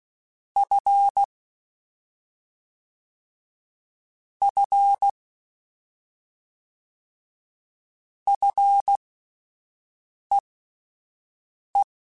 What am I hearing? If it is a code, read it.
Morse code, heard as a keyed tone, about 16 wpm, spaced as for 5 wpm, 800 Hz. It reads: F F FEE
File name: Fs / Single F mp3s / Single F Prefixes 3x2 Twice @5